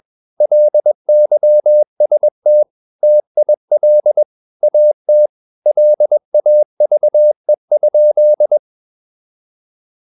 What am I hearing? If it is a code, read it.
Morse code, heard as a keyed tone, about 21 wpm, 600 Hz. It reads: LYST TIL AT LAVE?
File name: Ferie QRS 21wpm